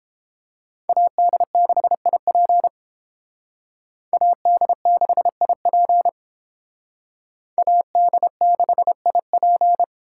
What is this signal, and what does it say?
Morse code, heard as a keyed tone, about 30 wpm, 700 Hz. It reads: AD6IP AD6IP AD6IP